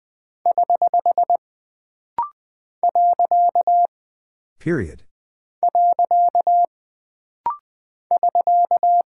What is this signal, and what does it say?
Morse code, heard as a keyed tone, about 20 wpm, 700 Hz. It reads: <HH> . <SK>
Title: Single Punctuation-Prosign - Rapid-Fire - 20wpm